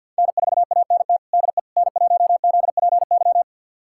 Morse code, heard as a keyed tone, about 51 wpm, 700 Hz. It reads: N4ANT DE N1ZPY
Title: USA Callsign DE Callsign - 5 Characters - 50wpm